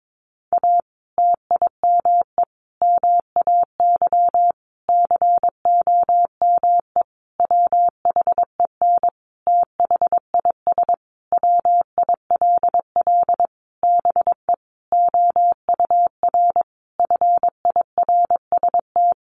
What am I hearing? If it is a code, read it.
Morse code, heard as a keyed tone, about 22 wpm, 700 Hz. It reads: A TIME MAY COME WHEN THIS WILL BE OUR FIRST
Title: Sentences from Top 1500 Words 22wpm